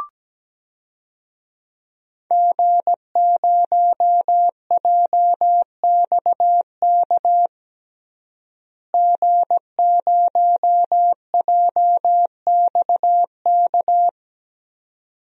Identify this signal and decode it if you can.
Morse code, heard as a keyed tone, about 17 wpm, 700 Hz. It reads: G0JXK G0JXK
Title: DXCC-100 Call Signs - 5 Characters - Repeated 2x - 17wpm